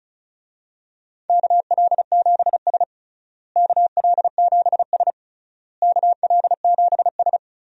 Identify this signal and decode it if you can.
Morse code, heard as a keyed tone, about 34 wpm, 700 Hz. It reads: KL7S KL7S KL7S
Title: DXCC-100 Call Signs - 4 Characters - Repeated 3x - 35wpm